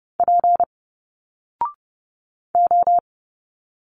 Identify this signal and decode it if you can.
Morse code, heard as a keyed tone, about 30 wpm, 700 Hz. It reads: P O
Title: Single Letters 30wpm